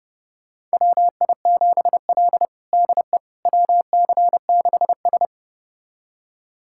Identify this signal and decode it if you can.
Morse code, heard as a keyed tone, about 30 wpm, 700 Hz. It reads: WI7L DE WC6S
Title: USA Callsign DE Callsign - 4 Characters - 30wpm